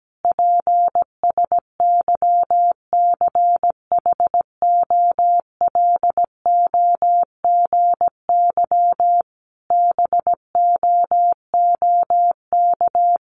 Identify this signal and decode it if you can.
Morse code, heard as a keyed tone, about 17 wpm, 700 Hz. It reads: PSYCHOLOGY BOOK
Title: Sentences from Top 1400 Words 17wpm